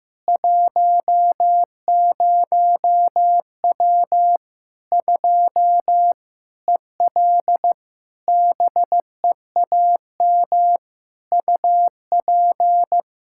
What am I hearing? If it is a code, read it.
Morse code, heard as a keyed tone, about 15 wpm, 700 Hz. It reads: 10W 2 EL BEAM UP
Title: Practice QSO - Exchange 2 - Info - 15wpm